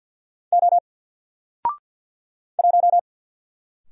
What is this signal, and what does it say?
Morse code, heard as a keyed tone, about 48 wpm, 700 Hz. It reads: O 1
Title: Single Letter-Number Alt 50wpm